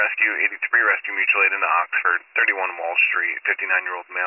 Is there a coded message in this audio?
Yes: Emergency Alert System (SAME) tones from 0.6 to 3.9 s, heard as harsh buzzing bursts, with speech.